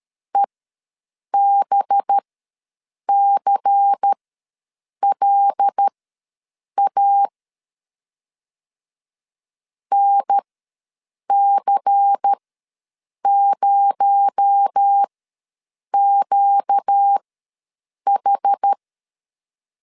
Morse code, heard as a keyed tone, about 13 wpm, spaced as for 7 wpm, 800 Hz. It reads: EBCLA NC0QH
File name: LEZIONI / 6 PKX0 / 6 PKX0 ws 12wpm 3448